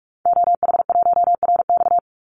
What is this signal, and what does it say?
Morse code, heard as a keyed tone, about 44 wpm, 700 Hz. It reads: OH1RX
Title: Call Signs - 45wpm